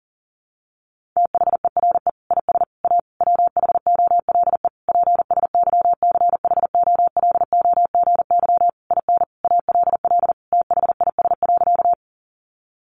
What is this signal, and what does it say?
Morse code, heard as a keyed tone, about 40 wpm, 700 Hz. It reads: THERE IS A WHOLE PSYCHOLOGY IN ALL THIS.
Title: Sentences from Top 1400 Words 40wpm